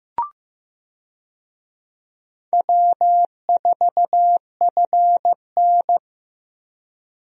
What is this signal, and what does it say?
Morse code, heard as a keyed tone, about 15 wpm, 700 Hz. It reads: W4FN
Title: USA Call Signs - 4 Characters - Repeated 2x - 15wpm